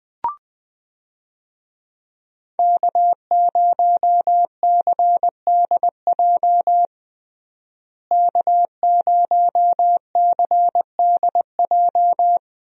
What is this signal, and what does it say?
Morse code, heard as a keyed tone, about 21 wpm, 700 Hz. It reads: K0CDJ K0CDJ
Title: USA Call Signs - 5 Characters - Repeated 2x - 20wpm